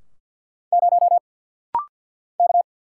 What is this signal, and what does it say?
Morse code, heard as a keyed tone, about 49 wpm, 690 Hz. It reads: 0 K
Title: Single Letter-Number - Rapid-Fire 50wpm